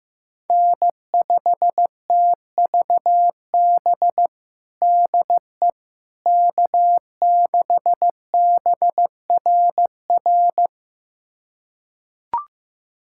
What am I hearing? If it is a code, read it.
Morse code, heard as a keyed tone, about 15 wpm, 700 Hz. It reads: N5TVB DE K6BRR
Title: USA Callsign DE Callsign - 5 Characters - 15wpm